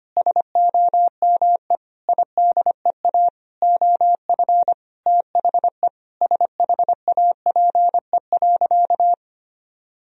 Morse code, heard as a keyed tone, about 25 wpm, 700 Hz. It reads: SOME IDEA OF THE SHAPE.
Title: Sentences from Top 1000 Words - Review Part 3 - 25wpm